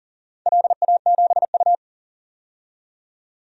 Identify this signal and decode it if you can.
Morse code, heard as a keyed tone, about 38 wpm, 700 Hz. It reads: LA7U
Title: DXCC-100 Call Signs - 4 Characters - 40wpm